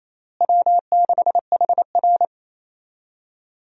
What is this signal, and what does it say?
Morse code, heard as a keyed tone, about 28 wpm, 700 Hz. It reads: W6HR
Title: USA Call Signs - 4 Characters - Repeated 3x - 28wpm